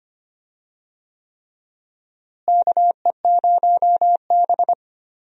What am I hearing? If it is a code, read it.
Morse code, heard as a keyed tone, about 25 wpm, 700 Hz. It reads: KE0B
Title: USA Call Signs - 4 Characters - Repeated 3x - 25wpm